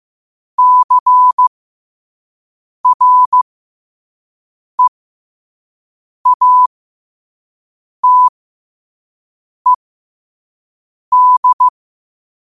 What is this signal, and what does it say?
Morse code, heard as a keyed tone, about 15 wpm, spaced as for 5 wpm, 1000 Hz. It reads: CREATED
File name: qotdfiles / 5wpm